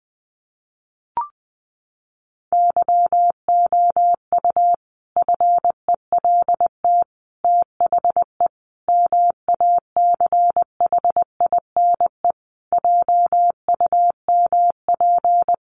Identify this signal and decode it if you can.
Morse code, heard as a keyed tone, about 20 wpm, 700 Hz. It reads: YOU FELT THE MACHINE JUMP
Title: A Sound Of Thunder 20wpm